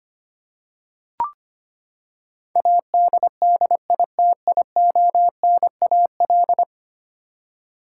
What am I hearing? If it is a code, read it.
Morse code, heard as a keyed tone, about 25 wpm, 700 Hz. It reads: ADDITIONAL
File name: Ten Letter Words - 25wpm